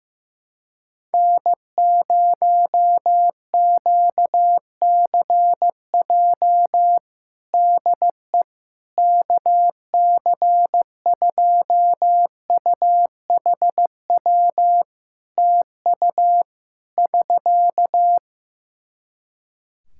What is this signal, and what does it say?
Morse code, heard as a keyed tone, about 15 wpm, 700 Hz. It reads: N0QCJ DE KC2UHW TU <SK>
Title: K1USN SST QSO Practice 15wpm